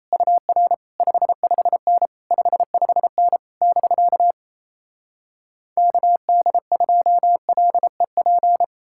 Morse code, heard as a keyed tone, about 37 wpm, 700 Hz. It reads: UR 55N 55N <BK> KD2LEP